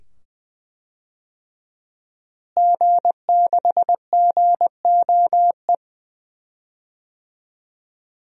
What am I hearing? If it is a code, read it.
Morse code, heard as a keyed tone, about 20 wpm, 700 Hz. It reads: G6GOE
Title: DXCC-100 Call Signs - 5 Characters - Repeated 2x - 20wpm